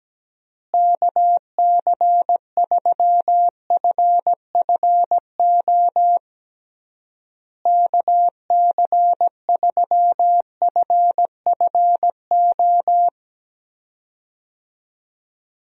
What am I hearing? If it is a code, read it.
Morse code, heard as a keyed tone, about 17 wpm, 700 Hz. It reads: KC3FFO KC3FFO
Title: USA Call Signs - 6 Characters - Repeated 3x - 17wpm